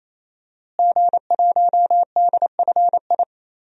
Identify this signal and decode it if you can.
Morse code, heard as a keyed tone, about 28 wpm, 700 Hz. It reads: G1DFI
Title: DXCC-100 Call Signs - 5 Characters - Repeated 3x - 28wpm